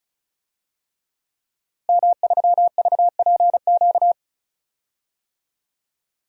Morse code, heard as a keyed tone, about 34 wpm, 700 Hz. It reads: M3VPQ
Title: DXCC-100 Call Signs - 5 Characters - Repeated 3x - 35wpm